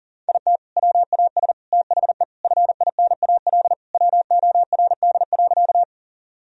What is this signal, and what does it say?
Morse code, heard as a keyed tone, about 37 wpm, 700 Hz. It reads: IT WAS THE FINAL WORD.
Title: Sentences from Top 1000 Words - Review Part 2 - 40wpm